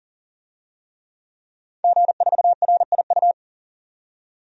Morse code, heard as a keyed tone, about 40 wpm, 700 Hz. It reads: G4RIU
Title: DXCC-100 Call Signs - 5 Characters - 40wpm